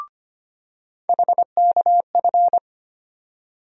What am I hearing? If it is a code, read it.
Morse code, heard as a keyed tone, about 25 wpm, 700 Hz. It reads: HKF
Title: USA Call Sign Postfixes - 3 Letters 25wpm